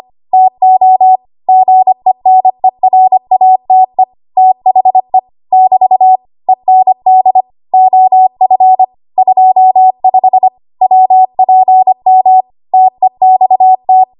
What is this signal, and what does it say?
Morse code, heard as a keyed tone, about 25 wpm, 750 Hz. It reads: TO GENERATE THE <BT> END OF 25 WPM TEXT
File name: Archive / 25 WPM / 25WPM